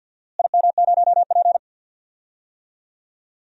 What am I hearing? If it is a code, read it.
Morse code, heard as a keyed tone, about 51 wpm, 700 Hz. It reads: IM0P
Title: DXCC-100 Call Signs - 4 Characters - Repeated 3x - 50wpm